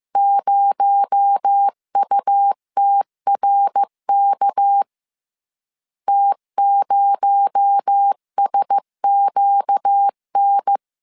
Morse code, heard as a keyed tone, about 15 wpm, 800 Hz. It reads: 0UTRK T0SQN